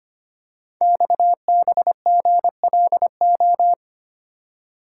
Morse code, heard as a keyed tone, about 25 wpm, 700 Hz. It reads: XBGLO
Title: Code Groups - 5 Letter:Numbers - 25wpm